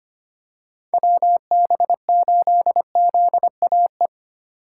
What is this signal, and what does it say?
Morse code, heard as a keyed tone, about 25 wpm, 700 Hz. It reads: WB8ZAE
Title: USA Call Signs - 2x Character Spacing - 25wpm